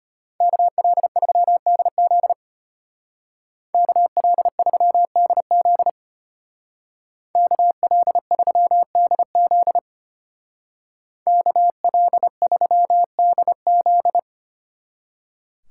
Morse code, heard as a keyed tone, about 30 wpm, 700 Hz. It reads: KL3DZ KL3DZ KL3DZ KL3DZ
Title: Speed Racing - USA 2x2 Call Signs 25wpm